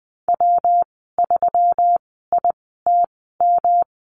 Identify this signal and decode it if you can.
Morse code, heard as a keyed tone, about 20 wpm, spaced as for 15 wpm, 700 Hz. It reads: W3ITM
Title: 161 Lesson 26g - Letter g - Call Signs 20wpm